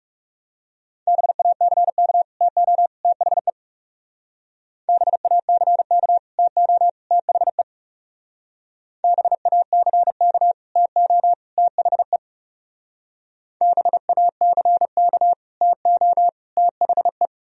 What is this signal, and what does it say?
Morse code, heard as a keyed tone, about 36 wpm, 700 Hz. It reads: BACK TO THE BACK TO THE BACK TO THE BACK TO THE
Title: Speed-Racing - Sets of 3 Words 30wpm